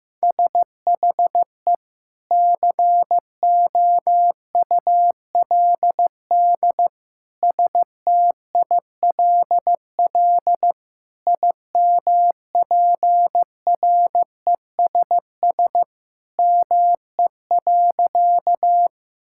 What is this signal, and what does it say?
Morse code, heard as a keyed tone, about 15 wpm, 700 Hz. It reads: SHE COULD STILL IMPRESS ME.
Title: Sentences from Top 1600 Words 15wpm